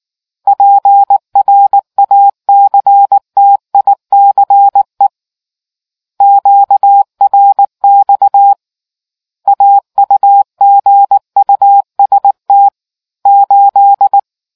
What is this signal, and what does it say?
Morse code, heard as a keyed tone, about 18 wpm, 800 Hz. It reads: PRACTICE QRX AUGUST 8